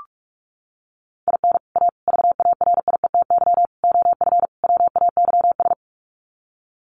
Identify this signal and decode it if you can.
Morse code, heard as a keyed tone, about 44 wpm, 700 Hz. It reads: IN A VARIETY OF WAYS
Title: Sets of 5 Words Encore 45wpm